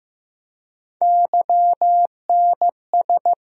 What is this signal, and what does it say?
Morse code, heard as a keyed tone, about 15 wpm, 700 Hz. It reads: YNS
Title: USA Call Sign Postfixes - 3 Letters 15wpm